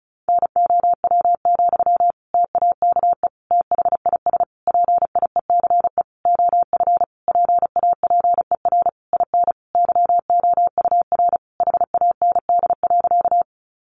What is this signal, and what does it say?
Morse code, heard as a keyed tone, about 34 wpm, 700 Hz. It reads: NOW, TAKE THIS PIECE OF PAPER IN YOUR HAND.
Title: Sentences from Top 400 Words 35wpm